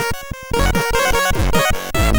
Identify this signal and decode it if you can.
There is no code.